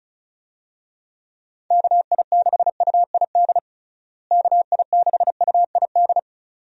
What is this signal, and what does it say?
Morse code, heard as a keyed tone, about 36 wpm, 700 Hz. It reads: KI6UID KI6UID
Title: USA Call Signs - 6 Characters - Repeated 2x - 35wpm